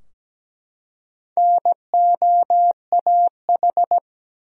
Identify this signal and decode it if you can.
Morse code, heard as a keyed tone, about 17 wpm, 700 Hz. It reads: NOAH
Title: USA Common Names - Max 4 Letters - 17wpm